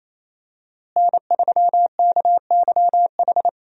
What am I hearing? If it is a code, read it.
Morse code, heard as a keyed tone, about 28 wpm, 700 Hz. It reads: N3KYH
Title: USA Call Signs - 5 Characters - Repeated 2x - 28wpm